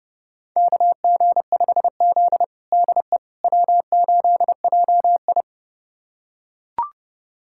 Morse code, heard as a keyed tone, about 30 wpm, 700 Hz. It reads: KG5Z DE W8JI
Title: USA Callsign DE Callsign - 4 Characters - 30wpm